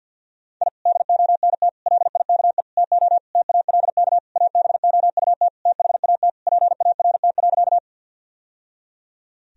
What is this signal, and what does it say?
Morse code, heard as a keyed tone, about 52 wpm, 700 Hz. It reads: I DONT LIKE TO TALK ABOUT THAT PART.
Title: Sentences from Top 1000 Words - Review Part 1 - 50wpm